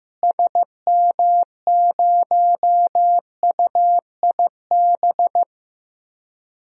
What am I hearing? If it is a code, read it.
Morse code, heard as a keyed tone, about 15 wpm, 700 Hz. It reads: SM0UIB
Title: DXCC-100 Call Signs - 6 Characters - Repeated 3x - 15wpm